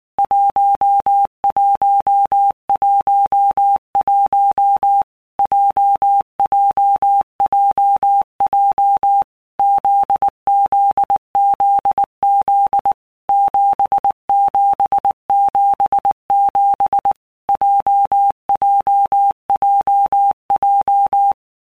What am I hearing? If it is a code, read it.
Morse code, heard as a keyed tone, about 19 wpm, 800 Hz. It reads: 1111 JJJJ ZZZZ 7777 JJJJ
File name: All Groups 1-minute Blocks / Groups@18 / JZ17